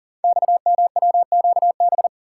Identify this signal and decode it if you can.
Morse code, heard as a keyed tone, about 40 wpm, 700 Hz. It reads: XMWQB
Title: Code Groups - 5 Letter:Numbers - 40wpm